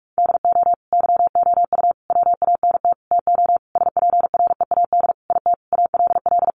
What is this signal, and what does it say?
Morse code, heard as a keyed tone, about 44 wpm, 700 Hz. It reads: DO YOU WANT TO SPREAD IT ALL
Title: Sentences from Top 1200 Words 45wpm